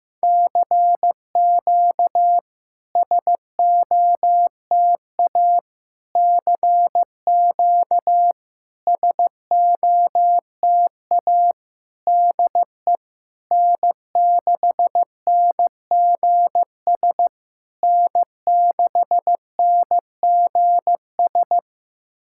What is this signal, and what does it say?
Morse code, heard as a keyed tone, about 15 wpm, 700 Hz. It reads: CQ SOTA CQ SOTA DE N6NGS N6NGS
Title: SOTA QSO Practice 15wpm